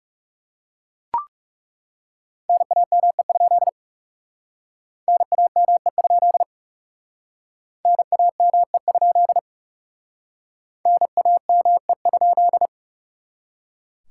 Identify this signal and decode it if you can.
Morse code, heard as a keyed tone, about 40 wpm, 700 Hz. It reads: NAME? NAME? NAME? NAME?